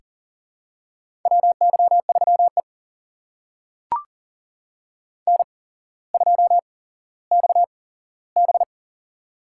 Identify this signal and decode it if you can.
Morse code, heard as a keyed tone, about 38 wpm, spaced as for 17 wpm, 700 Hz. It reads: WY3E N2XB
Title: USA Call Signs - 4 Characters - 8x Character Spacing - 40wpm